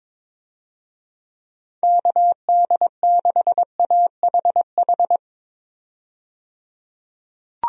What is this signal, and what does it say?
Morse code, heard as a keyed tone, about 22 wpm, 700 Hz. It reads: KD6AHH
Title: USA Call Signs - 6 Characters - 22wpm